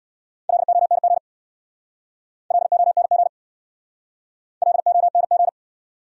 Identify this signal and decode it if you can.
Morse code, heard as a keyed tone, about 66 wpm, 700 Hz. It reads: LONG LONG LONG